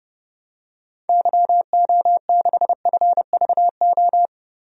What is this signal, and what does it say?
Morse code, heard as a keyed tone, about 32 wpm, 700 Hz. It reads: YO6FVO
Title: DXCC-100 Call Signs - 6 Characters - Repeated 3x - 30wpm